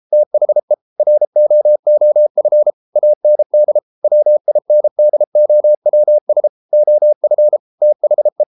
Morse code, heard as a keyed tone, about 33 wpm, 600 Hz. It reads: THE ROOF AND WINDOWS OF THE